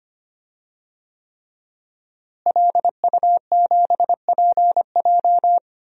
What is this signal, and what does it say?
Morse code, heard as a keyed tone, about 25 wpm, 700 Hz. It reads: LU7PJ
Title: DXCC-100 Call Signs - 5 Characters - Repeated 3x - 25wpm